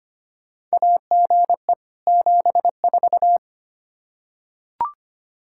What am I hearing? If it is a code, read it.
Morse code, heard as a keyed tone, about 25 wpm, 700 Hz. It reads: AGE 74
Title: Practice QSO - Exchange 3 - More Info - 25wpm